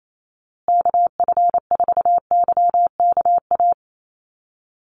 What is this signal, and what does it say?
Morse code, heard as a keyed tone, about 28 wpm, 700 Hz. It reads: KF4YKA
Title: USA Call Signs Repeated 2x 28wpm